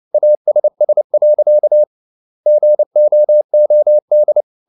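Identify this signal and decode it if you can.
Morse code, heard as a keyed tone, about 29 wpm, 600 Hz. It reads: ASS. GOOD
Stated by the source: Morse code at 29 wpm, sending Ken Follett's novel Never in English, from CW med Gnister.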